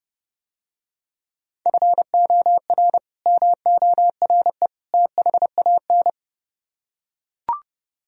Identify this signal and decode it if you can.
Morse code, heard as a keyed tone, about 31 wpm, 700 Hz. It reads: FOR MORE THAN